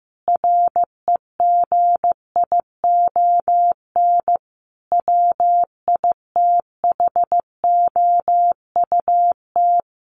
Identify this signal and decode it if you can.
Morse code, heard as a keyed tone, about 14 wpm, 700 Hz. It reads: REGION WITHOUT
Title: Sentences from Top 1500 Words 15wpm